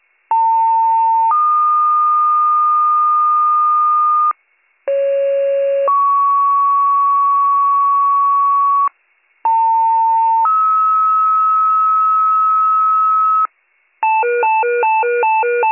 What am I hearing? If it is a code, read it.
Two-tone page, heard as a one-second tone, then a three-second tone, about 900 then 1220 Hz, about 550 then 1120 Hz, about 900 then 1290 Hz. It is heard from 0.3 to 13.5 s.